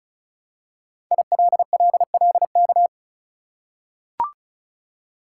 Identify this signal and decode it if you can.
Morse code, heard as a keyed tone, about 36 wpm, 700 Hz. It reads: ILLLK
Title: Code Groups - 5 Letters - 35wpm